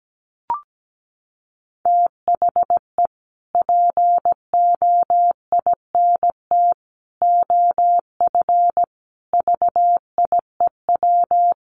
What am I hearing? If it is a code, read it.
Morse code, heard as a keyed tone, about 17 wpm, 700 Hz. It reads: THE POINT OF VIEW
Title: Sets of 5 Words 17wpm